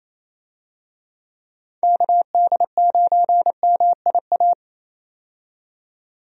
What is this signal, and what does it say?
Morse code, heard as a keyed tone, about 28 wpm, 700 Hz. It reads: KD9MIA
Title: USA Call Signs - 6 Characters - 28wpm